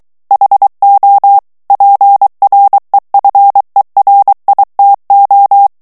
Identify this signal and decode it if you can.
Morse code, heard as a keyed tone, about 23 wpm, 800 Hz. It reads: HO PREFERITO
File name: morse mp3 / ITALIEN texte clair 1 vitesse 22 mots minute